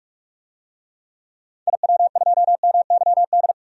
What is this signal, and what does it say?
Morse code, heard as a keyed tone, about 43 wpm, 700 Hz. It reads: IW2MYD
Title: DXCC-100 Call Signs - 6 Characters - Repeated 2x - 45wpm